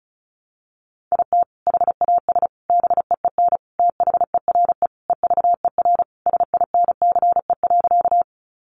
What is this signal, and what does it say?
Morse code, heard as a keyed tone, about 34 wpm, 700 Hz. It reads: IT HAS BEEN THERE EVER SINCE.
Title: Sentences from Top 400 Words 35wpm